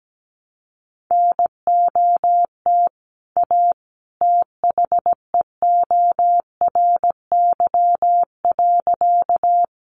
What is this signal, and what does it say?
Morse code, heard as a keyed tone, about 17 wpm, 700 Hz. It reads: NOT A THEORY.
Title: Sentences from Top 600 Words 17wpm